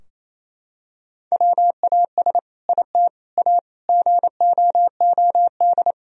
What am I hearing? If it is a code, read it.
Morse code, heard as a keyed tone, about 28 wpm, 700 Hz. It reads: WAS IT A GOOD
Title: Sentences from Top 1000 Words - Review Part 4 - 28wpm